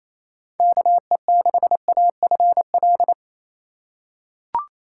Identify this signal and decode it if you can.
Morse code, heard as a keyed tone, about 28 wpm, 700 Hz. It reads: KE6AFL
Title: USA Call Signs - 6 Characters - 4x Character Spacing - 28wpm